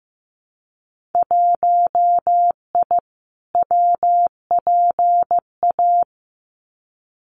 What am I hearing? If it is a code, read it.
Morse code, heard as a keyed tone, about 15 wpm, 700 Hz. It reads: 1I WPA
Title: Winter Field Day Exchanges Encore 15wpm